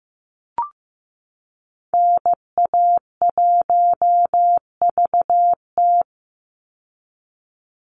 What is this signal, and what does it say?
Morse code, heard as a keyed tone, about 15 wpm, 700 Hz. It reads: NA1VT
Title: Call Signs Repeated 2x - 15wpm